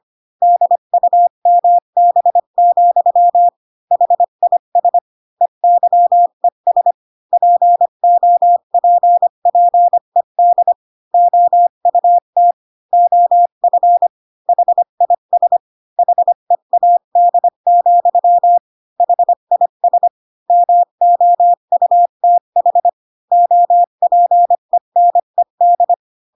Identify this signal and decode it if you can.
Morse code, heard as a keyed tone, about 25 wpm, 700 Hz. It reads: DUMB, HIS EYES POPPED OUT OF HIS HEAD, HIS MOUTH OPENED
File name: Pinocchio / Pinocchio ch1 20 25 ik1hge